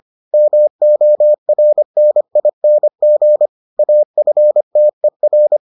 Morse code, heard as a keyed tone, about 25 wpm, 600 Hz. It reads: MORNING AFTER